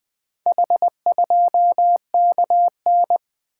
Morse code, heard as a keyed tone, about 20 wpm, 700 Hz. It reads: H2KN
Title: Four Letter-Number Combinations - 20wpm